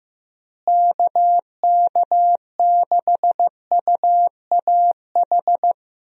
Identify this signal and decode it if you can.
Morse code, heard as a keyed tone, about 15 wpm, 700 Hz. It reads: KK6UAH
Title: USA Call Signs - 6 Characters - Repeated 3x - 15wpm